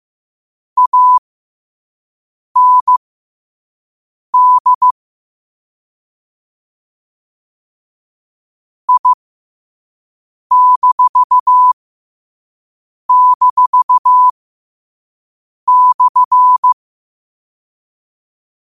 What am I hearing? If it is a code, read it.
Morse code, heard as a keyed tone, about 15 wpm, spaced as for 5 wpm, 1000 Hz. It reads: AND I--/
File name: qotdfiles / 5wpm